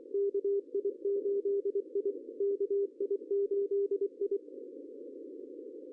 Morse code, heard as a keyed tone, about 27 wpm, 410 Hz. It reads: KI8I KI8I
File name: CW / cqww-cw-k5tr